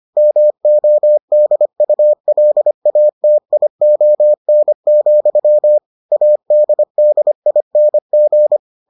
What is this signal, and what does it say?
Morse code, heard as a keyed tone, about 25 wpm, 600 Hz. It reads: MODULATION, ADDING